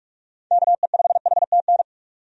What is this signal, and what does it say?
Morse code, heard as a keyed tone, about 42 wpm, 700 Hz. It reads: KE5HTN
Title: USA Call Signs - 6 Characters - 45wpm